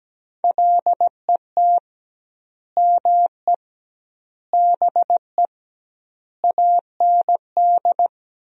Morse code, heard as a keyed tone, about 17 wpm, 700 Hz. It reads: LET ME BE AND
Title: 2x Word-Spacing - Sentences from Top 300 Words - 17wpm